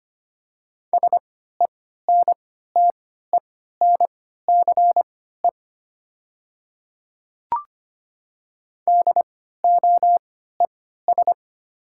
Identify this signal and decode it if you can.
Morse code, heard as a keyed tone, about 25 wpm, spaced as for 14 wpm, 700 Hz. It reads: SENTENCE DOES
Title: 3x Character Spacing - Top 200 Words - 25wpm